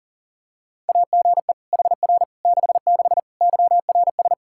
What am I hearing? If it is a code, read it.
Morse code, heard as a keyed tone, about 40 wpm, 700 Hz. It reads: AGE HR 66 YRS
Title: Practice QSO - Exchange 3 - More Info - 40wpm